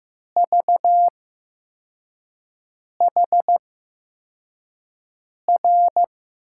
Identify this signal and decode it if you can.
Morse code, heard as a keyed tone, about 15 wpm, spaced as for 4 wpm, 700 Hz. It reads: VHR
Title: USA Call Signs - 6 Characters - 8x Character Spacing - 15wpm